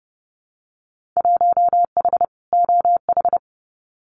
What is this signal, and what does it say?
Morse code, heard as a keyed tone, about 30 wpm, 700 Hz. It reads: 1H OH
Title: Winter Field Day Exchanges 30wpm